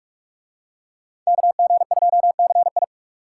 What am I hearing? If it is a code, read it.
Morse code, heard as a keyed tone, about 41 wpm, 700 Hz. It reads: KG2CI